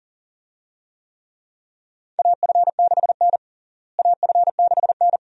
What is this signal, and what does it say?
Morse code, heard as a keyed tone, about 37 wpm, 700 Hz. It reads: AF6N AF6N
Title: USA Call Signs - 4 Characters - Repeated 2x - 40wpm